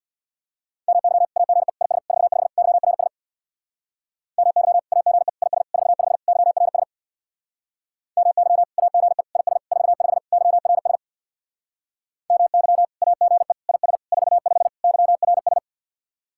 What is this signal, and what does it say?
Morse code, heard as a keyed tone, about 66 wpm, 700 Hz. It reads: MY AGE IS 45 YRS MY AGE IS 45 YRS MY AGE IS 45 YRS MY AGE IS 45 YRS